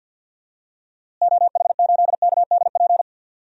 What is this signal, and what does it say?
Morse code, heard as a keyed tone, about 52 wpm, 700 Hz. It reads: OH8KDP
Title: DXCC-100 Call Signs - 6 Characters - Repeated 2x - 50wpm